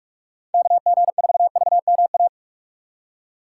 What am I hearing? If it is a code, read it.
Morse code, heard as a keyed tone, about 46 wpm, 700 Hz. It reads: KG4VMA
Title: USA Call Signs - 6 Characters - 3x Character Spacing - 45wpm